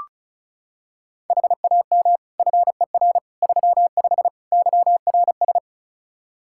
Morse code, heard as a keyed tone, about 36 wpm, 700 Hz. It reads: HAM FER 35 YRS
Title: K1USN SST QSO Practice 35wpm